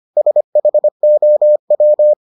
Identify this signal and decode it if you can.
Morse code, heard as a keyed tone, about 25 wpm, 600 Hz. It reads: SHOW